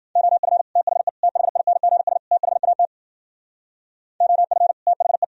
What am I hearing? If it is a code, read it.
Morse code, heard as a keyed tone, about 49 wpm, 700 Hz. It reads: OF THE THINGS THAT OF THE
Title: Speed-Racing - Sets of 4 Words 40wpm